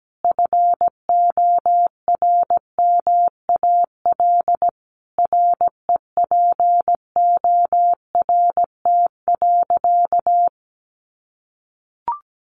Morse code, heard as a keyed tone, about 17 wpm, 700 Hz. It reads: FORMAL REPORT.
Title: Sentences from Top 1300 Words 17wpm